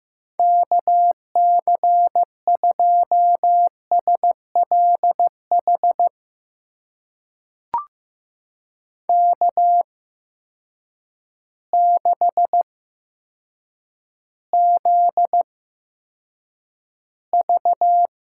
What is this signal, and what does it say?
Morse code, heard as a keyed tone, about 15 wpm, spaced as for 8 wpm, 700 Hz. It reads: KC2SLH K6ZV
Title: USA Call Signs - 8x Character Spacing - 15wpm